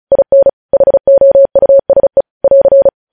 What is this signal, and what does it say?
Morse code, heard as a keyed tone, about 36 wpm, 550 Hz. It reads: IN HOUSE <AR>
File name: podcast / News headlines-1x-35x35